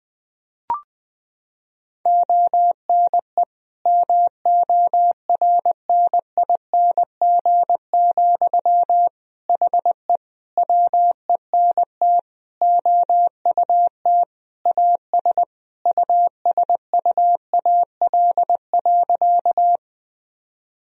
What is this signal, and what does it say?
Morse code, heard as a keyed tone, about 20 wpm, 700 Hz. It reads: ONE MORNING, HE WENT OUT AS USUAL.
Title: Sentences from Top 1000 Words - Review Part 2 - 20wpm